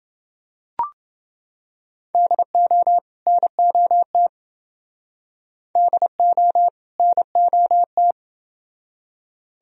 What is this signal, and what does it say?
Morse code, heard as a keyed tone, about 28 wpm, 700 Hz. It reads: DO NOT DO NOT